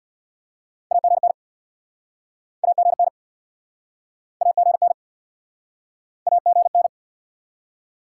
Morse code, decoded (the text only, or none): AGN AGN AGN AGN